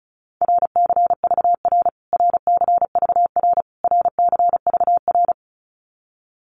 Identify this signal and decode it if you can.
Morse code, heard as a keyed tone, about 34 wpm, 700 Hz. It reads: RCVR RCVR RCVR
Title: Abbreviations Repeated 3x - 35wpm